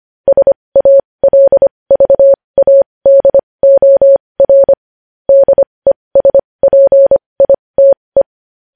Morse code, heard as a keyed tone, about 25 wpm, spaced as for 20 wpm, 550 Hz. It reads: SALVADOR DESPITE
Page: News Headlines in Morse Code at 20 WPM